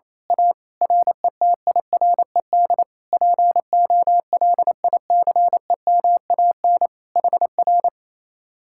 Morse code, heard as a keyed tone, about 26 wpm, 700 Hz. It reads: A RETIRED POLICEMAN HR
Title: Speed-Racing Common QSO Phrases Encore - 28wpm